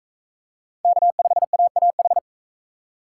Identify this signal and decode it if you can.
Morse code, heard as a keyed tone, about 43 wpm, 700 Hz. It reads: K5AAH